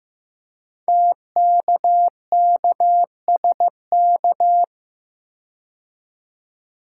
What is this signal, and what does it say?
Morse code, heard as a keyed tone, about 15 wpm, 700 Hz. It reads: TKKSK
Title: Code Groups - 5 Letters - 15wpm